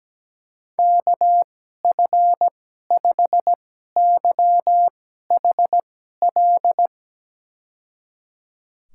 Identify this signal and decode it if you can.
Morse code, heard as a keyed tone, about 17 wpm, spaced as for 12 wpm, 700 Hz. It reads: KF5YHL
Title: USA Call Signs - 6 Characters - 2x Character Spacing - 17wpm